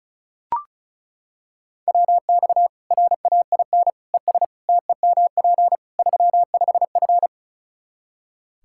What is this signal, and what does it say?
Morse code, heard as a keyed tone, about 36 wpm, 700 Hz. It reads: WX RAIN ES TEMP 35F
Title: Practice QSO - Exchange 2 - Info - 35wpm